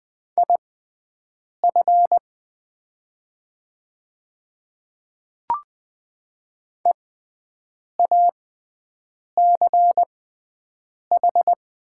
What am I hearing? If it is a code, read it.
Morse code, heard as a keyed tone, about 20 wpm, spaced as for 7 wpm, 700 Hz. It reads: IF EACH
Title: 6x Character Spacing - Top 100 Words - 20wpm